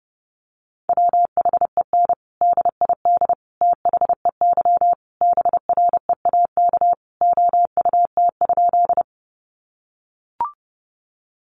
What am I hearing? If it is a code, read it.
Morse code, heard as a keyed tone, about 30 wpm, 700 Hz. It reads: WHEN DID THEY BREAK OUT?
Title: Sentences from Top 900 Words 30wpm